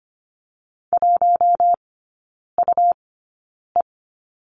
Morse code, heard as a keyed tone, about 25 wpm, spaced as for 9 wpm, 700 Hz. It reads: 1UE
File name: Lesson 17e - Letter u - Three Characters 25wpm